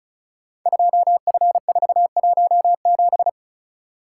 Morse code, heard as a keyed tone, about 36 wpm, 690 Hz. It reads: 2F417
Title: Code Groups - 5 Letter:Numbers - 35wpm